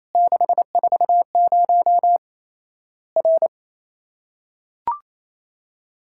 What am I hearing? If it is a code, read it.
Morse code, heard as a keyed tone, about 28 wpm, 700 Hz. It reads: 640 R
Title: Worked All Europe DX Contest - QTC Exchange - 28wpm